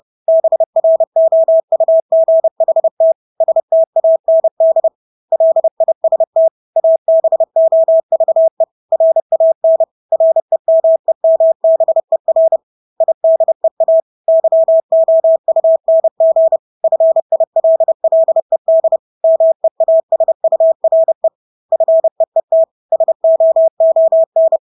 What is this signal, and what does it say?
Morse code, heard as a keyed tone, about 30 wpm, 650 Hz. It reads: BROUGHT STAND LIST ABOVE RAN REMEMBER IDEA YOUNG FILLED MEASURE FEET SOON
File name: Engelske ord 30-30 wpm